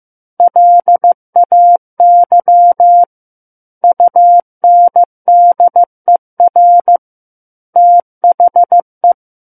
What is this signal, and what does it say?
Morse code, heard as a keyed tone, about 15 wpm, 700 Hz. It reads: LAY UNDER THE
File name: Book One / 06-Chapter 6 - 15 WPM